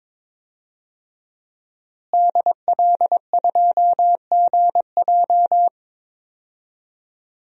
Morse code, heard as a keyed tone, about 22 wpm, 700 Hz. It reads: DL2GJ
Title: DXCC-100 Call Signs - 5 Characters - Repeated 2x - 22wpm